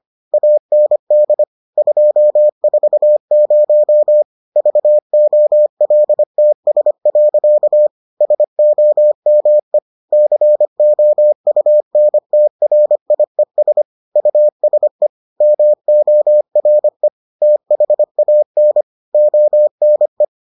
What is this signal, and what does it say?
Morse code, heard as a keyed tone, about 25 wpm, 600 Hz. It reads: AND 240 VOLTS. SOME COUNTRIES USE MORE THAN ONE